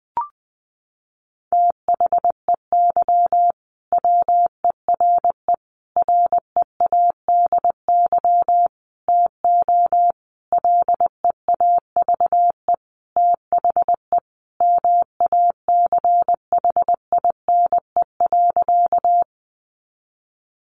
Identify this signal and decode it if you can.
Morse code, heard as a keyed tone, about 20 wpm, 700 Hz. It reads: THEY WERE READY TO LEAVE THE MACHINE.
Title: A Sound Of Thunder 20wpm